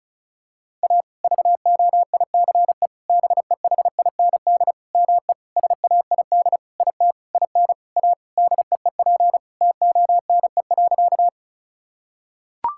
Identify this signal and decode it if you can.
Morse code, heard as a keyed tone, about 36 wpm, 700 Hz. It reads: A VOICE BEHIND ME SAID IT IN A DEEP TONE.
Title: Sentences from Top 1600 Words 35wpm